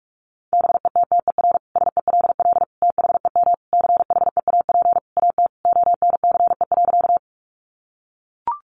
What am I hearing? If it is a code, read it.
Morse code, heard as a keyed tone, about 44 wpm, 700 Hz. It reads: BETTER SELL THEM CHEAP AT ONCE.
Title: Sentences from Top 1200 Words 45wpm